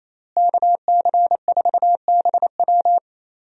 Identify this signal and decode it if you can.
Morse code, heard as a keyed tone, about 28 wpm, 700 Hz. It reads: KC4BW
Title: USA Call Signs - 5 Characters - Repeated 3x - 28wpm